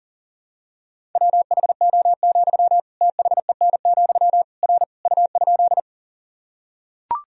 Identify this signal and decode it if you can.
Morse code, heard as a keyed tone, about 40 wpm, 700 Hz. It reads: WHO, THEN, R U?
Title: Sentences from Top 100 Words and Abbreviations 40wpm